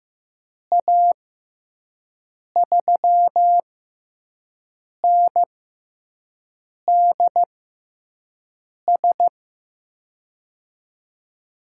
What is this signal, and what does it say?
Morse code, heard as a keyed tone, about 15 wpm, spaced as for 5 wpm, 700 Hz. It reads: A3NDS
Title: USA Call Signs - 6x Character Spacing - 15wpm